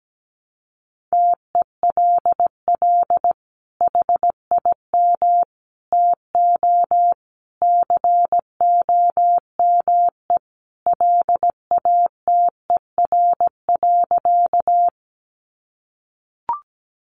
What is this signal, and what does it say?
Morse code, heard as a keyed tone, about 17 wpm, 700 Hz. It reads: TELL HIM TO COME LATER.
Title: Sentences from Top 700 Words 17wpm